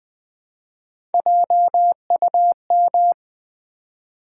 Morse code, heard as a keyed tone, about 20 wpm, 700 Hz. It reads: JUM
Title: Build A Word - 1 Letter At A Time - 4-Letter Words - 20wpm